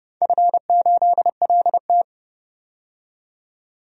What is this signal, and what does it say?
Morse code, heard as a keyed tone, about 30 wpm, 700 Hz. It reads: F8LT
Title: DXCC-100 Call Signs - 4 Characters - 30wpm